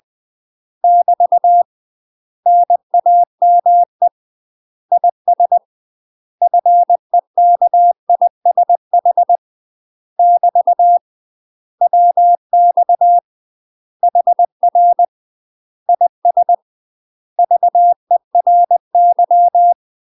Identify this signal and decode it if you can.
Morse code, heard as a keyed tone, about 20 wpm, 700 Hz. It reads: <BT> NAME IS FEKISH <BT> WX HR IS VERY
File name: UK MASTER QSOs TRAINING / 20WPM / UK MASTER QSOs 0011